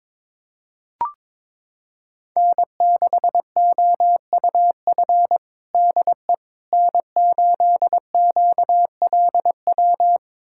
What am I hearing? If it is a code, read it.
Morse code, heard as a keyed tone, about 22 wpm, 700 Hz. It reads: N6OUF DE N8QLW
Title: USA Callsign DE Callsign - 5 Characters - 22wpm